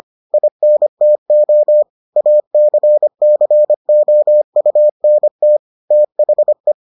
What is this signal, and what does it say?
Morse code, heard as a keyed tone, about 25 wpm, 600 Hz. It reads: INTO ACCOUNT THE